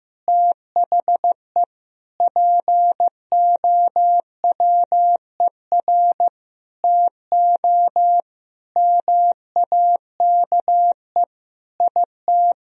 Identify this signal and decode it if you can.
Morse code, heard as a keyed tone, about 15 wpm, 700 Hz. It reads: THE POWER TO MAKE IT
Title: Sentences from Top 1000 Words - Review Part 3 - 15wpm